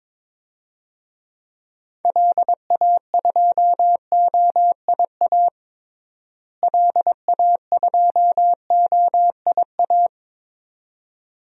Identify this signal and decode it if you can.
Morse code, heard as a keyed tone, about 21 wpm, 700 Hz. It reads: LA2OIA LA2OIA
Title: DXCC-100 Call Signs - 6 Characters - Repeated 2x - 22wpm